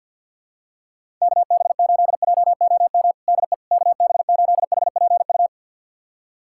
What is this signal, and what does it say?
Morse code, heard as a keyed tone, about 52 wpm, 700 Hz. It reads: KB8JOM DE KB8HPU
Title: USA Callsign DE Callsign - 6 Characters - 50wpm